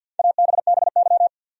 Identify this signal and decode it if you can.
Morse code, heard as a keyed tone, about 46 wpm, 700 Hz. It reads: ABBY
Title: USA Common Names - Max 5 Letters - 50wpm